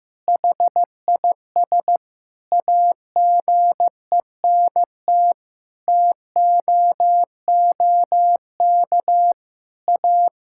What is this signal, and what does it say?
Morse code, heard as a keyed tone, about 15 wpm, 700 Hz. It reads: HIS AGENT TOOK A
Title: Sentences from Top 1600 Words 15wpm